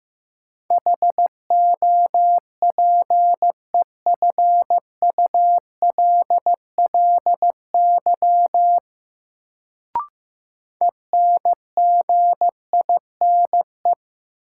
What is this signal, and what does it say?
Morse code, heard as a keyed tone, about 15 wpm, 700 Hz. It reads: HOPEFULLY ENGINE
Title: Sentences from Top 1500 Words - New Word Drill Alt - 15wpm